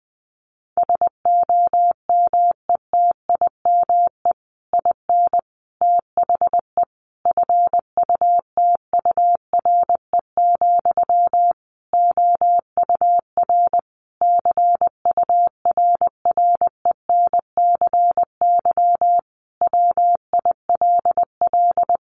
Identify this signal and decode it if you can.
Morse code, heard as a keyed tone, about 20 wpm, 700 Hz. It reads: SOMETIME IN THE FUTURE, OUR CURRENCY WILL
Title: Sentences from Top 1500 Words 20wpm